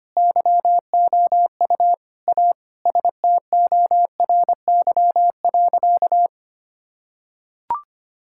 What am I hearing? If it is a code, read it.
Morse code, heard as a keyed tone, about 25 wpm, 700 Hz. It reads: YOU A STORY.
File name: practice / 25wpm / Sentences with 4-Word Phrases - 25wpm